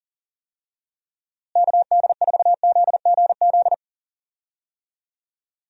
Morse code, heard as a keyed tone, about 38 wpm, 700 Hz. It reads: KD4ZGZ
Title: USA Call Signs - 6 Characters - Repeated 3x - 40wpm